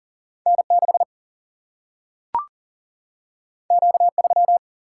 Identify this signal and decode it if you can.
Morse code, heard as a keyed tone, about 40 wpm, 700 Hz. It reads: N6 Q3
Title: Single-Letter Single-Number 40wpm